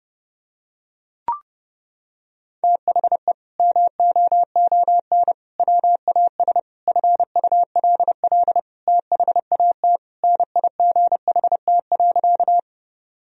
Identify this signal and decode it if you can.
Morse code, heard as a keyed tone, about 32 wpm, 700 Hz. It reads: THE MOON WAS FULL THAT NIGHT.
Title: Sentences from Top 1000 Words - Review Part 3 - 30wpm